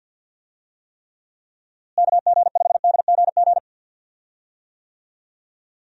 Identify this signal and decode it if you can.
Morse code, heard as a keyed tone, about 51 wpm, 700 Hz. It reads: KG5DGG
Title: USA Call Signs - 6 Characters - Repeated 3x - 50wpm